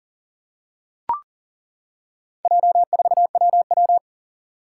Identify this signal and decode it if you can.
Morse code, heard as a keyed tone, about 40 wpm, 690 Hz. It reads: J4WW
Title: Four Challenging Letter-Number Combinations 40wpm